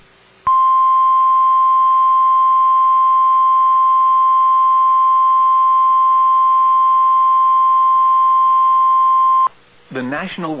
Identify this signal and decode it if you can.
Emergency Alert System (SAME) tones (the 1050 Hz attention tone), with speech. It is heard from 0.4 to 9.5 s.